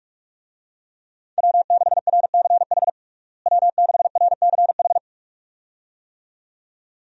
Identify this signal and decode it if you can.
Morse code, heard as a keyed tone, about 46 wpm, 700 Hz. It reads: W6RCH W6RCH
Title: USA Call Signs - 5 Characters - Repeated 2x - 45wpm